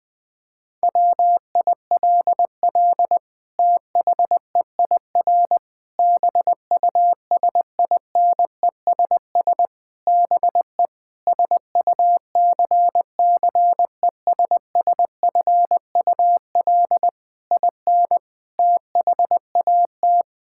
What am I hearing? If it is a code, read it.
Morse code, heard as a keyed tone, about 20 wpm, 700 Hz. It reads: WILL THEIR BUSINESS BE SUCCESSFUL IN THAT
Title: Sentences from Top 1000 Words - Review Part 5 - 20wpm